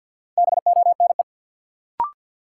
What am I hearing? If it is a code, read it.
Morse code, heard as a keyed tone, about 53 wpm, 700 Hz. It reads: BONE
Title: Sentences from Top 1600 Words - New Word Drill Alt - 50wpm